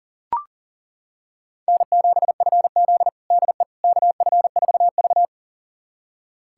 Morse code, heard as a keyed tone, about 39 wpm, 700 Hz. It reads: N7FZ DE KF4V
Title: USA Callsign DE Callsign - 4 Characters - 40wpm